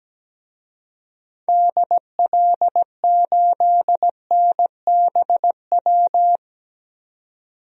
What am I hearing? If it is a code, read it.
Morse code, heard as a keyed tone, about 17 wpm, 700 Hz. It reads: DL8NBW
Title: DXCC-100 Call Signs - 6 Characters - 17wpm